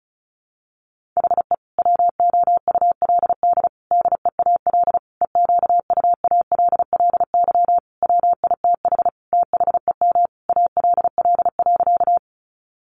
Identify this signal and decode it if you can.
Morse code, heard as a keyed tone, about 34 wpm, 700 Hz. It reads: HE WOULD DEAL EQUALLY WITH THEM ALL.
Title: Sentences from Top 1400 Words 35wpm